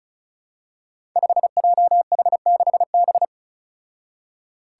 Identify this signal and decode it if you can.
Morse code, heard as a keyed tone, about 36 wpm, 700 Hz. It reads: 5JH6B
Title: Five Challenging Letter-Number Combinations 35wpm